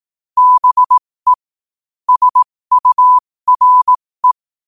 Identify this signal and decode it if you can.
Morse code, heard as a keyed tone, about 18 wpm, spaced as for 15 wpm, 1000 Hz. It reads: BE SURE
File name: qotdfiles / 15wpm